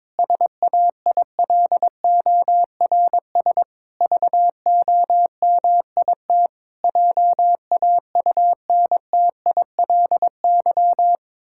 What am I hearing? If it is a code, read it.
Morse code, heard as a keyed tone, about 22 wpm, 700 Hz. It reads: SAILORS VOMIT JAUNTILY
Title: Pangram Challenge v2 - 22wpm